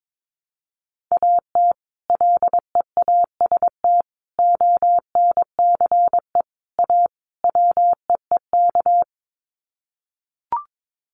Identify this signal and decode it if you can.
Morse code, heard as a keyed tone, about 22 wpm, 700 Hz. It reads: AT LEAST ONCE A WEEK
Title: Sets of 5 Words Encore2 22wpm